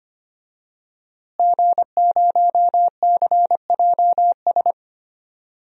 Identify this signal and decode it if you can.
Morse code, heard as a keyed tone, about 25 wpm, 700 Hz. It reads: G0CJS
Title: DXCC-100 Call Signs - 5 Characters - Repeated 2x - 25wpm